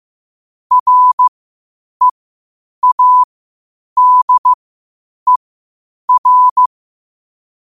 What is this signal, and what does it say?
Morse code, heard as a keyed tone, about 15 wpm, spaced as for 8 wpm, 1000 Hz. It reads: READER
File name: qotdfiles / 8wpm